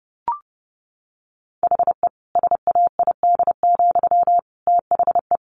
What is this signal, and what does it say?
Morse code, heard as a keyed tone, about 30 wpm, 700 Hz. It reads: HE SAID, THE
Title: Sentences from Top 400 Words 30wpm